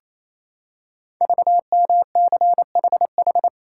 Morse code, heard as a keyed tone, about 28 wpm, 700 Hz. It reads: VMCHH
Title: Code Groups - 5 Letter:Numbers - 28wpm